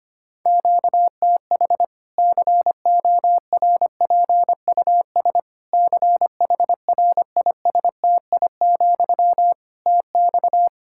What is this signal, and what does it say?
Morse code, heard as a keyed tone, about 25 wpm, 700 Hz. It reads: QTH CORPUS CHRISTI, TX
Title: K1USN SST QSO Practice 25wpm